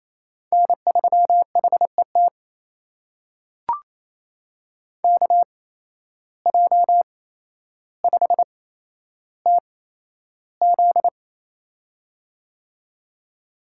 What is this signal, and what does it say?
Morse code, heard as a keyed tone, about 28 wpm, spaced as for 12 wpm, 700 Hz. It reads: N3HET KJ5TZ
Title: USA Call Signs - 5 Characters - 8x Character Spacing - 28wpm